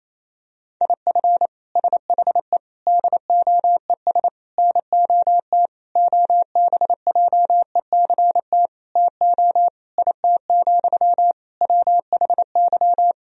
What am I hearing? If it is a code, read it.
Morse code, heard as a keyed tone, about 28 wpm, 700 Hz. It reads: IF SHE DOES NOT OBJECT TO IT, WHY
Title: Sentences from Top 1000 Words - Review Part 3 - 28wpm